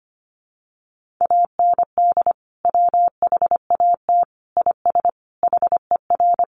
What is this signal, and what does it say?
Morse code, heard as a keyed tone, about 25 wpm, 700 Hz. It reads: AND WHAT IS HER
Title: Sentences from Top 200 Words 25wpm